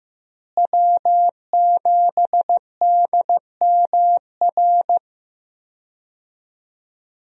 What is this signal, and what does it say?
Morse code, heard as a keyed tone, about 15 wpm, 700 Hz. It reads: W7DMR
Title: USA Call Signs - 5 Characters - Repeated 3x - 15wpm